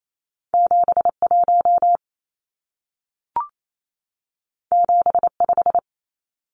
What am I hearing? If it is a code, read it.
Morse code, heard as a keyed tone, about 28 wpm, 700 Hz. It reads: 71 75
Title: Numbers - 2-Digit 28wpm